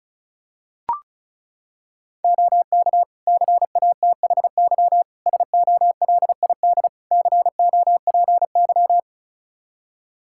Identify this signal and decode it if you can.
Morse code, heard as a keyed tone, about 34 wpm, 700 Hz. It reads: OK CATHY SOLID COPY
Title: K1USN SST QSO Practice 35wpm